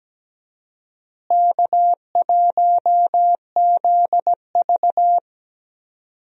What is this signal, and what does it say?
Morse code, heard as a keyed tone, about 17 wpm, 700 Hz. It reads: K1ZV
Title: USA Call Signs - 4 Characters - Repeated 2x - 17wpm